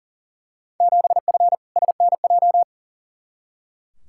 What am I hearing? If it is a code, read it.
Morse code, heard as a keyed tone, about 40 wpm, 700 Hz. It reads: 7F SNJ
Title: Field Day Exchanges Encore - 40wpm